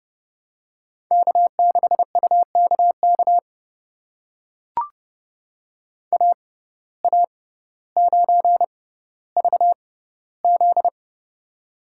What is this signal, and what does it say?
Morse code, heard as a keyed tone, about 30 wpm, spaced as for 15 wpm, 700 Hz. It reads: K6UKK AA9VZ
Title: USA Call Signs - 5 Characters - 6x Character Spacing - 30wpm